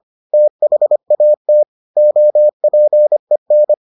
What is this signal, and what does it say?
Morse code, heard as a keyed tone, about 25 wpm, 600 Hz. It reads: THAT OPEN